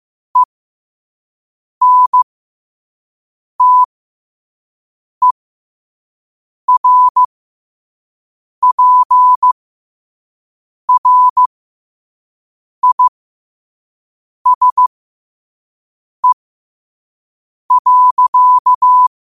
Morse code, heard as a keyed tone, about 15 wpm, spaced as for 5 wpm, 1000 Hz. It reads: ENTERPRISE.